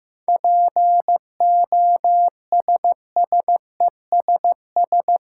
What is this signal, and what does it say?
Morse code, heard as a keyed tone, about 15 wpm, 700 Hz. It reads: POSSESS
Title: Seven Letter Words - 15wpm